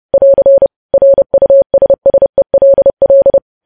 Morse code, heard as a keyed tone, about 30 wpm, 550 Hz. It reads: <AR> RUSSELL
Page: news headlines in morse code at 30 wpm